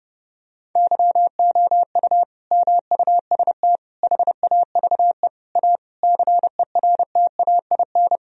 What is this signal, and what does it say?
Morse code, heard as a keyed tone, about 30 wpm, 700 Hz. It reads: YOU MUST HAVE A CERTAIN
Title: Sentences from Top 1000 Words - Review Part 4 - 30wpm